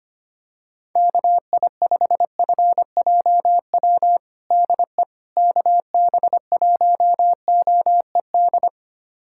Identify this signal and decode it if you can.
Morse code, heard as a keyed tone, about 25 wpm, 700 Hz. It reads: KI5FJW DE KB1OED
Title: USA Callsign DE Callsign - 6 Characters - 25wpm